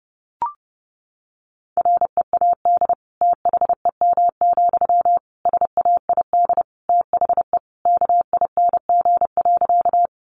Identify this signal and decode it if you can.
Morse code, heard as a keyed tone, about 30 wpm, 700 Hz. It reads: READ THEM, SAID THE KING.
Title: Sentences from Top 400 Words 30wpm